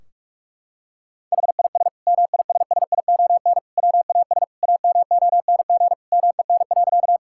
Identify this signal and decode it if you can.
Morse code, heard as a keyed tone, about 44 wpm, 700 Hz. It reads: HIS MISSION WAS AMONG MEN.
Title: Sentences from Top 1600 Words 45wpm